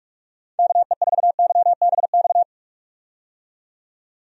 Morse code, heard as a keyed tone, about 46 wpm, 700 Hz. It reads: KE4YBX
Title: USA Call Signs - 6 Characters - 45wpm